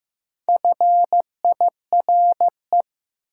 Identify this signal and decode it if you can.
Morse code, heard as a keyed tone, about 15 wpm, 700 Hz. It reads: FIRE
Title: Sentences from Top 400 Words - New Word Drill - 15wpm